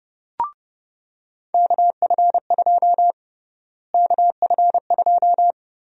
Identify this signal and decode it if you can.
Morse code, heard as a keyed tone, about 29 wpm, 700 Hz. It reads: KF2 KF2
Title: USA Call Sign Prefixes - Repeated 2x - 30wpm